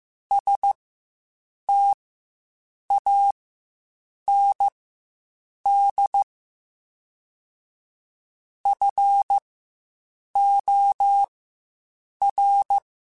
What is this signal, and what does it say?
Morse code, heard as a keyed tone, about 15 wpm, spaced as for 7 wpm, 800 Hz. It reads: STAND FOR
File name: Quotes Spaced / Quotes Spaced-mp3s / StandFor sp @10